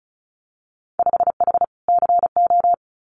Morse code, heard as a keyed tone, about 36 wpm, 700 Hz. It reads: 5H CO
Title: Winter Field Day Exchanges Encore 35wpm